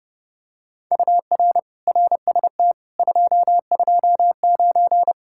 Morse code, heard as a keyed tone, about 31 wpm, 700 Hz. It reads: UR RST 229